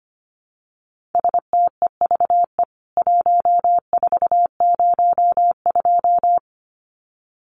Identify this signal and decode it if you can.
Morse code, heard as a keyed tone, about 25 wpm, 700 Hz. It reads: STEVE 1402
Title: CWT Exchange 25wpm